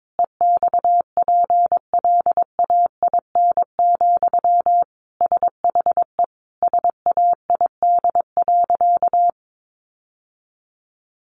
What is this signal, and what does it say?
Morse code, heard as a keyed tone, about 22 wpm, 700 Hz. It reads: EXPLAIN, SHE SAID.
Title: Sentences from Top 1100 Words 22wpm